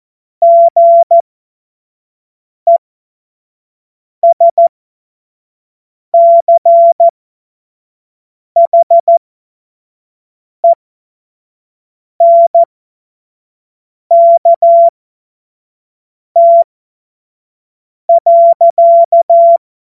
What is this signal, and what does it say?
Morse code, heard as a keyed tone, about 14 wpm, spaced as for 5 wpm, 680 Hz. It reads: GESCHENKT.
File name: Morsekurs